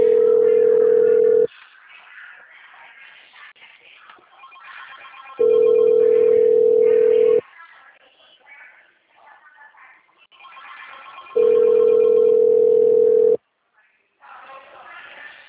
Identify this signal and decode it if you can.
Ringback tone (North America), 440 + 480 Hz.
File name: out- -2050- - - .6649